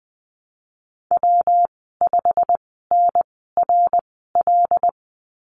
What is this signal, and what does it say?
Morse code, heard as a keyed tone, about 19 wpm, spaced as for 14 wpm, 700 Hz. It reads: W5NRL
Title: 128 Lesson 21g - Number 6 - Call Signs 20wpm